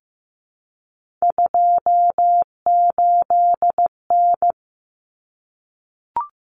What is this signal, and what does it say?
Morse code, heard as a keyed tone, about 15 wpm, 700 Hz. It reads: 28N
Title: Common Cut Numbers - 3-Digit 15wpm